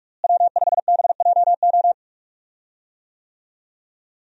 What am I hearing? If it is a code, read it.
Morse code, heard as a keyed tone, about 44 wpm, 700 Hz. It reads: W5BJO
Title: USA Call Signs - 5 Characters - Repeated 3x - 45wpm